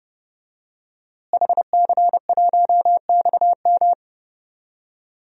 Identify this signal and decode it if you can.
Morse code, heard as a keyed tone, about 31 wpm, 700 Hz. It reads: HC1XM
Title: DXCC-100 Call Signs - 5 Characters - Repeated 3x - 30wpm